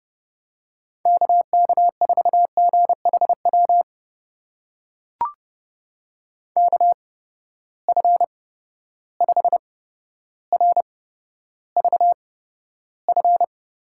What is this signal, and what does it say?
Morse code, heard as a keyed tone, about 30 wpm, spaced as for 13 wpm, 700 Hz. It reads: KK4GHW KF5RVF
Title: USA Call Signs - 6 Characters - 8x Character Spacing - 30wpm